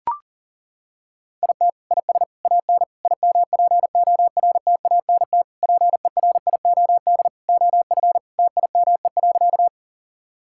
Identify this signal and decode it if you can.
Morse code, heard as a keyed tone, about 40 wpm, 700 Hz. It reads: IT IS AN IMPORTANT PERIOD OF TIME.
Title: Sentences from Top 1000 Words - Review Part 5 - 40wpm